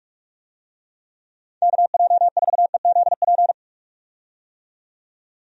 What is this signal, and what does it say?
Morse code, heard as a keyed tone, about 46 wpm, 700 Hz. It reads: KJ4EZP
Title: USA Call Signs - 6 Characters - 45wpm